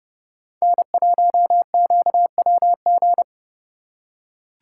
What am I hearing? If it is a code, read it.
Morse code, heard as a keyed tone, about 29 wpm, 700 Hz. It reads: N1QWG